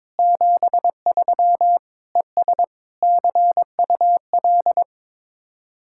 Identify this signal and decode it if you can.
Morse code, heard as a keyed tone, about 23 wpm, 700 Hz. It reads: 73 ES CUL
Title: Speed-Racing Common QSO Phrases Encore - 22wpm